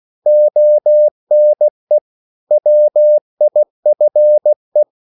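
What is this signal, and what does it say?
Morse code, heard as a keyed tone, about 16 wpm, 600 Hz. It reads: ONE WIFE
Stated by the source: English quotations sent in CW as practice at 16 words per minute.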